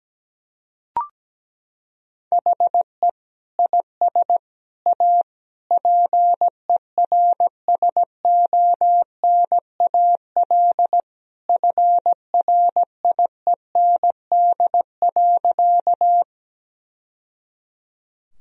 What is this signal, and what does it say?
Morse code, heard as a keyed tone, about 17 wpm, 700 Hz. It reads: HE IS A PERSONAL FRIEND.
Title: Sentences from Top 1000 Words - Review Part 4 - 17wpm